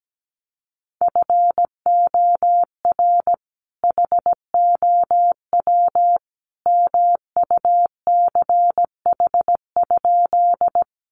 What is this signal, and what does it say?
Morse code, heard as a keyed tone, about 17 wpm, 700 Hz. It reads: FOR HOW MUCH?
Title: Sentences from Top 200 Words 17wpm